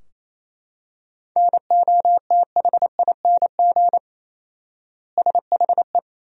Morse code, heard as a keyed tone, about 28 wpm, 700 Hz. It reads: NOTHING SHE
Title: 4x Word-Spacing - Sentences from Top 500 Words - 28wpm